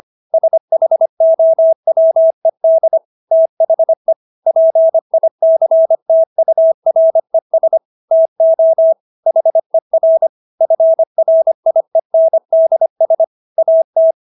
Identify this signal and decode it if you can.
Morse code, heard as a keyed tone, about 25 wpm, 650 Hz. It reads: SHOWED THE PICTURES TO HER FRIENDS AT